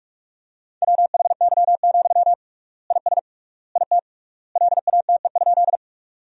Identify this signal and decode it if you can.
Morse code, heard as a keyed tone, about 46 wpm, 700 Hz. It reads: WHY, IS IT LATE?
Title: 3x Word-Spacing - Sentences from Top 300 Words - 45wpm